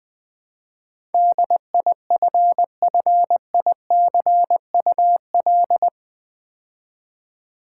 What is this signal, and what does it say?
Morse code, heard as a keyed tone, about 20 wpm, 700 Hz. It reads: DIFFICUL
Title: Build A Word - 4 Letters At A Time - 10-Letter Words - 20wpm